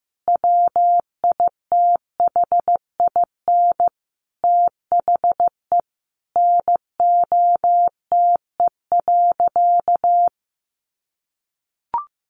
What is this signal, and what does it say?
Morse code, heard as a keyed tone, about 15 wpm, 700 Hz. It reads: WITHIN THE NOTE.
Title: Sentences from Top 1500 Words 15wpm